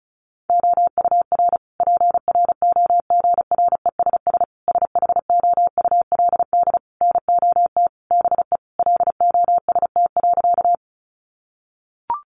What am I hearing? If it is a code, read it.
Morse code, heard as a keyed tone, about 34 wpm, 700 Hz. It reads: OUR PROGRESS SHOULD NOT BE LOST.
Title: Sentences from Top 1400 Words 35wpm